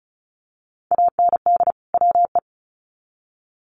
Morse code, heard as a keyed tone, about 34 wpm, 700 Hz. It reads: AND WE
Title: Sets of 2 Words 35wpm